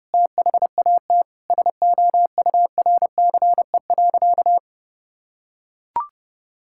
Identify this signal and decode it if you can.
Morse code, heard as a keyed tone, about 30 wpm, 700 Hz. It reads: THAT SOURCE.
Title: Sentences from Top 1000 Words - Review Part 5 - 30wpm